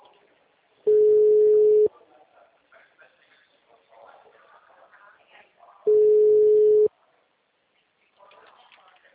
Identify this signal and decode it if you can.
Ringback tone (Europe, 425 Hz), with speech.